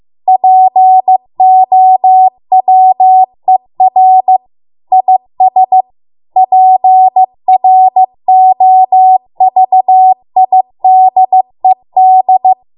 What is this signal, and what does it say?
Morse code, heard as a keyed tone, about 15 wpm, 750 Hz. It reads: POWER IS PROVIDED